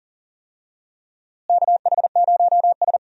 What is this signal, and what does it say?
Morse code, heard as a keyed tone, about 39 wpm, 700 Hz. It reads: KH0S
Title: USA Call Signs - 4 Characters - Repeated 3x - 40wpm